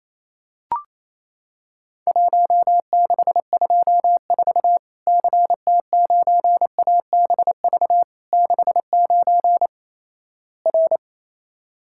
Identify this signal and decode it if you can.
Morse code, heard as a keyed tone, about 27 wpm, 700 Hz. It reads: 1624 CT9ABV 69 R
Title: Worked All Europe DX Contest - QTC Exchange - 28wpm